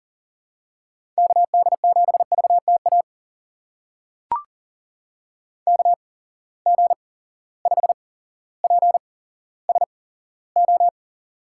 Morse code, heard as a keyed tone, about 40 wpm, spaced as for 17 wpm, 700 Hz. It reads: KD7VTA KG5PSO
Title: USA Call Signs - 6 Characters - 8x Character Spacing - 40wpm